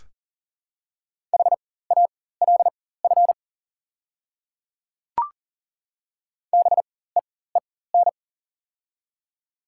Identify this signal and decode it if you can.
Morse code, heard as a keyed tone, about 40 wpm, spaced as for 19 wpm, 700 Hz. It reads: HALF BEEN
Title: 4x Character Spacing - Top 500 Words - 40wpm